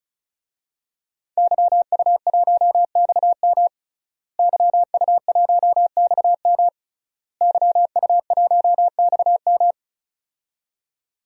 Morse code, heard as a keyed tone, about 36 wpm, 700 Hz. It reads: YU1XM YU1XM YU1XM
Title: DXCC-100 Call Signs - 5 Characters - Repeated 3x - 35wpm